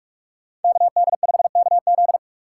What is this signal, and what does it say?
Morse code, heard as a keyed tone, about 46 wpm, 700 Hz. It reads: KD5KZ
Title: USA Call Signs - 5 Characters - 2x Character Spacing - 45wpm